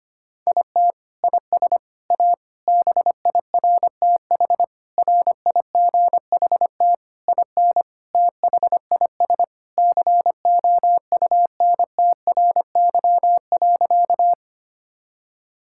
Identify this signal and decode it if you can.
Morse code, heard as a keyed tone, about 25 wpm, 700 Hz. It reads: IT IS A BIRTH RIGHT IN THIS COUNTRY.
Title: Sentences from Top 1000 Words - Review Part 5 - 25wpm